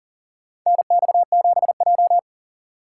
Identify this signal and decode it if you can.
Morse code, heard as a keyed tone, about 40 wpm, 700 Hz. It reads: NX7J
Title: USA Call Signs - 6x Character Spacing - 40wpm